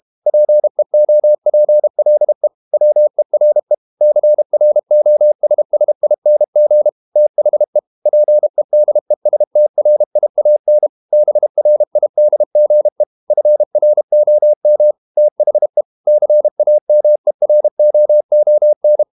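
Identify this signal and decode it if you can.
Morse code, heard as a keyed tone, about 32 wpm, 600 Hz. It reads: PEOPLE WERE CROSSING THE PEDESTRIAN BRIDGE FROM THE CAMEROON
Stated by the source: Morse code practice at 32 wpm from CW med Gnister, reading Ken Follett's novel Never in English.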